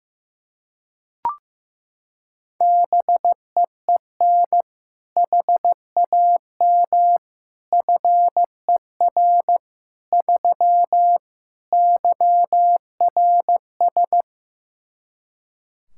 Morse code, heard as a keyed tone, about 15 wpm, 700 Hz. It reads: BEEN HAM FER 3 YRS
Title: Practice QSO - Exchange 3 - More Info - 15wpm